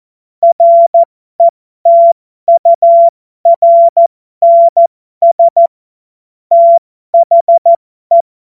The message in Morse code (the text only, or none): RETURNS THE